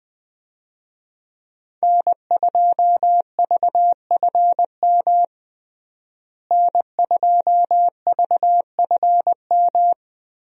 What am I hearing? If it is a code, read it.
Morse code, heard as a keyed tone, about 20 wpm, 700 Hz. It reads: N2VFM N2VFM
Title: USA Call Signs - 5 Characters - Repeated 3x - 20wpm